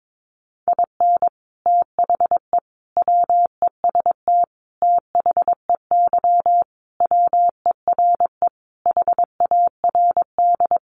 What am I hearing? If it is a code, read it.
Morse code, heard as a keyed tone, about 22 wpm, 700 Hz. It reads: IN THE WEST THEY WERE HARD
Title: Sentences from Top 1300 Words 22wpm